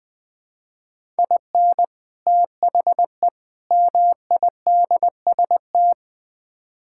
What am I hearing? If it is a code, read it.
Morse code, heard as a keyed tone, about 20 wpm, 700 Hz. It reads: IN THE MIDST
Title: Build A Phrase - 5-Word Phrases - 20wpm